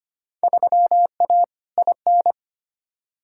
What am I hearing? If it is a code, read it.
Morse code, heard as a keyed tone, about 25 wpm, 700 Hz. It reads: 3A IN
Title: Field Day Exchanges - 25wpm